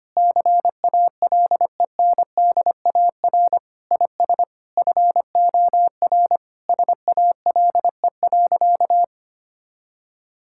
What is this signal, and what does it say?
Morse code, heard as a keyed tone, about 25 wpm, 700 Hz. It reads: CALENDAR IS FOR SALE.
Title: Sentences from Top 1600 Words 25wpm